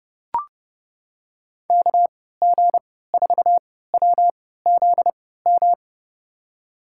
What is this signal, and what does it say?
Morse code, heard as a keyed tone, about 30 wpm, spaced as for 17 wpm, 700 Hz. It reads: KG4WZM
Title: USA Call Signs - 6 Characters - 3x Character Spacing - 30wpm